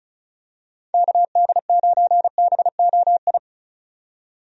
Morse code, heard as a keyed tone, about 34 wpm, 700 Hz. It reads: KD9BOI
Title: USA Call Signs - 6 Characters - 35wpm